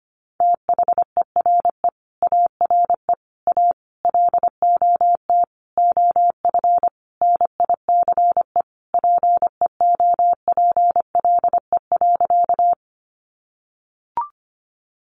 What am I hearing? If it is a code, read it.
Morse code, heard as a keyed tone, about 25 wpm, 700 Hz. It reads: THERE ARE A LOT OF NICE PEOPLE.
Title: Sentences from Top 1100 Words 25wpm